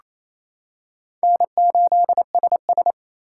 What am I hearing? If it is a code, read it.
Morse code, heard as a keyed tone, about 28 wpm, 700 Hz. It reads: N8SS
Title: K1USN SST QSO Practice - 28wpm